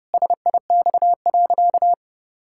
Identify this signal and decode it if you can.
Morse code, heard as a keyed tone, about 30 wpm, 700 Hz. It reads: SIX.
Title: Pangram Challenge v2 - 30wpm